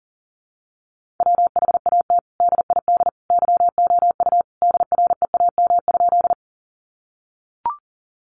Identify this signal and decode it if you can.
Morse code, heard as a keyed tone, about 40 wpm, 700 Hz. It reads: WHAT DID YOU DREAM?
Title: Sentences from Top 1500 Words 40wpm